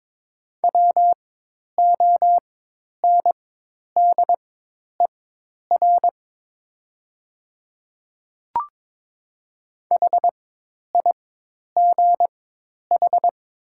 Morse code, heard as a keyed tone, about 22 wpm, spaced as for 10 wpm, 700 Hz. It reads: WONDER HIGH
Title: 4x Character Spacing - Top 500 Words - 22wpm